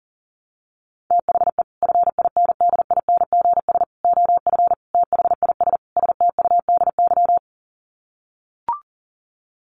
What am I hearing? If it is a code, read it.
Morse code, heard as a keyed tone, about 40 wpm, 700 Hz. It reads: THE FINDINGS OF THIS STUDY
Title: Sets of 5 Words Encore2 40wpm